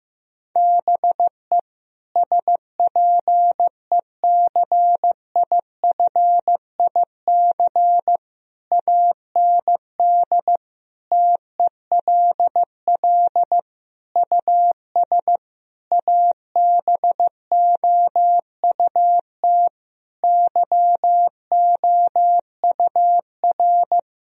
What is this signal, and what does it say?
Morse code, heard as a keyed tone, about 15 wpm, 700 Hz. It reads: BE SPECIFIC AND TELL US ABOUT YOUR
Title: Sentences from Top 1000 Words - Review Part 3 - 15wpm